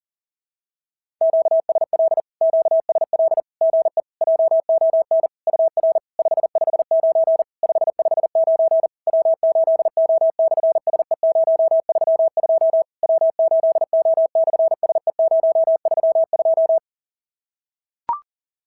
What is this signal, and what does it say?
Morse code, heard as a keyed tone, about 40 wpm, 650 Hz. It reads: QSL QSL GE JON UR 559 559 W8O/SE032 W8O/SE032
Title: SOTA QSO Practice 40wpm